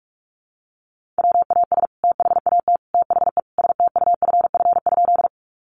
Morse code, heard as a keyed tone, about 44 wpm, 700 Hz. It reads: WAS THAT THE STUFF?
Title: Sentences from Top 1400 Words 45wpm